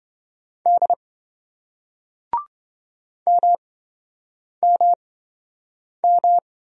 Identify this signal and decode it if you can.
Morse code, heard as a keyed tone, about 25 wpm, spaced as for 7 wpm, 700 Hz. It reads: D MMM
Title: Speed-Racing Single Letters 20wpm